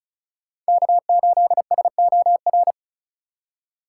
Morse code, heard as a keyed tone, about 34 wpm, 700 Hz. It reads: K8SOR
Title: USA Call Signs - 3x Character Spacing - 35wpm